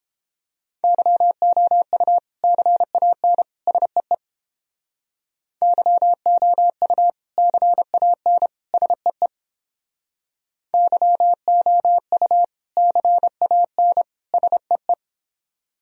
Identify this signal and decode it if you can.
Morse code, heard as a keyed tone, about 30 wpm, 700 Hz. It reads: YOU CAN SEE YOU CAN SEE YOU CAN SEE